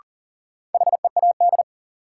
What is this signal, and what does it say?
Morse code, heard as a keyed tone, about 40 wpm, 700 Hz. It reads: HEAD
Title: Sentences from Top 300 Words - New Word Drill - 40wpm